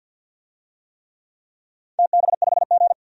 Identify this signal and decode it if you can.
Morse code, heard as a keyed tone, about 52 wpm, 700 Hz. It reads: TB5G
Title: DXCC-100 Call Signs - 4 Characters - 50wpm